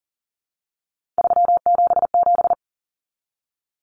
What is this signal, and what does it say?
Morse code, heard as a keyed tone, about 40 wpm, 700 Hz. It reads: 377
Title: Numbers - 3-Digit 40wpm